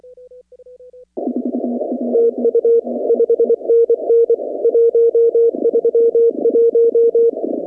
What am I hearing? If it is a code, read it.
Morse code, heard as a keyed tone, about 24 wpm, 460 Hz. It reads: TU 5NN 131
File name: contests / 2019-cw-wpx-kt5j-k5pi- -2330 0-15